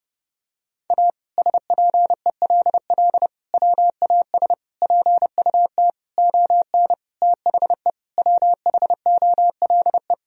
Morse code, heard as a keyed tone, about 30 wpm, 700 Hz. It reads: A SPELL WAS PUT ON THE WHOLE
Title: Sentences from Top 1000 Words - Review Part 1 - 30wpm